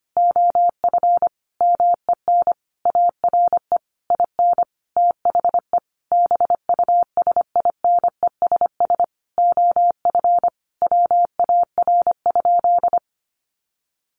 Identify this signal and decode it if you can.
Morse code, heard as a keyed tone, about 25 wpm, 700 Hz. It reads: OF MEN ARE IN THE BUSINESS OF WAR?
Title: Sentences from Top 1300 Words 25wpm